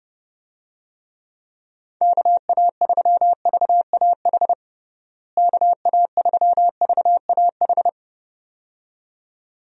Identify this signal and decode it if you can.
Morse code, heard as a keyed tone, about 30 wpm, 700 Hz. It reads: KA3VAH KA3VAH
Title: USA Call Signs - 6 Characters - Repeated 2x - 30wpm